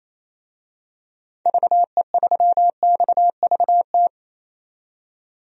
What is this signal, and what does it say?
Morse code, heard as a keyed tone, about 28 wpm, 700 Hz. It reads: VE3XVT
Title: DXCC-100 Call Signs - 6 Characters - 28wpm